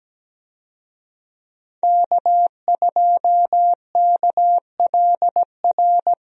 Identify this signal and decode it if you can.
Morse code, heard as a keyed tone, about 17 wpm, 700 Hz. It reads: K2KLR
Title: USA Call Signs - 5 Characters - 17wpm